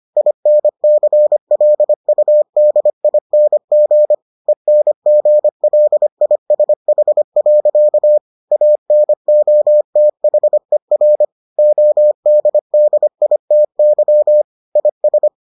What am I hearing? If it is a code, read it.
Morse code, heard as a keyed tone, about 25 wpm, 600 Hz. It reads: INCLUDING ENGLISH. ANOTHER ODDITY IS